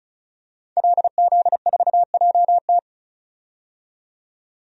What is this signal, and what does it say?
Morse code, heard as a keyed tone, about 34 wpm, 700 Hz. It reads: LZ4JT
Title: DXCC-100 Call Signs - 5 Characters - 35wpm